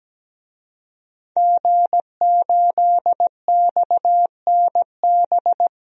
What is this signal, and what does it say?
Morse code, heard as a keyed tone, about 17 wpm, 700 Hz. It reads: G8XNB
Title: DXCC-100 Call Signs - 5 Characters - Repeated 2x - 17wpm